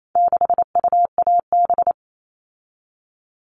Morse code, heard as a keyed tone, about 28 wpm, 700 Hz. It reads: 6UAB
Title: All Cut Numbers - 4-Digit 28wpm